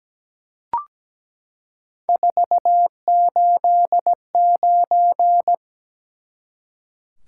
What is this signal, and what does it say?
Morse code, heard as a keyed tone, about 17 wpm, 700 Hz. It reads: 489
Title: Top 180 QSO Elements 17wpm